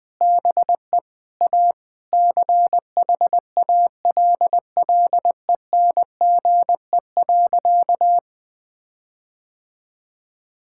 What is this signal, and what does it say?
Morse code, heard as a keyed tone, about 20 wpm, 700 Hz. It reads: BE A CHALLENGE.
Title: Sentences with 5-Word Phrases - 20wpm